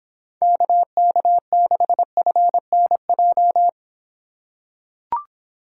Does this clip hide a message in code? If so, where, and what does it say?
Yes: Morse code from 0.4 to 3.7 s, heard as a keyed tone, about 25 wpm, 700 Hz. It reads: KK6FNJ